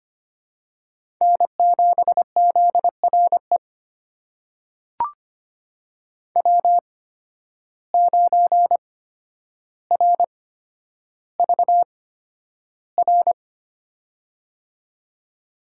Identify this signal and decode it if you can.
Morse code, heard as a keyed tone, about 25 wpm, spaced as for 11 wpm, 700 Hz. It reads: N7ZRE W9RVR
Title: USA Call Signs - 5 Characters - 8x Character Spacing - 25wpm